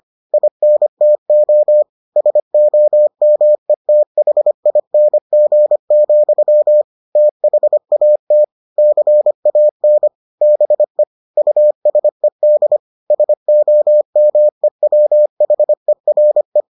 Morse code, heard as a keyed tone, about 25 wpm, 600 Hz. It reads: INTO SOMETHING, THAT CAN BE USED SOMEWHERE